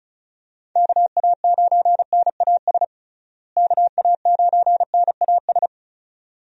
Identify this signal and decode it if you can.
Morse code, heard as a keyed tone, about 34 wpm, 700 Hz. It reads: KA9NAS KA9NAS
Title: USA Call Signs - 6 Characters - Repeated 2x - 35wpm